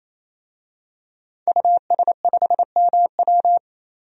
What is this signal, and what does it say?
Morse code, heard as a keyed tone, about 28 wpm, 700 Hz. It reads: US5MW
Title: DXCC-100 Call Signs - 5 Characters - 28wpm